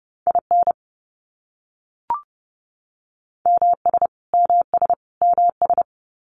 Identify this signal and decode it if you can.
Morse code, heard as a keyed tone, about 30 wpm, 700 Hz. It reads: IN MS MS MS
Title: US States Repeated 3x 30wpm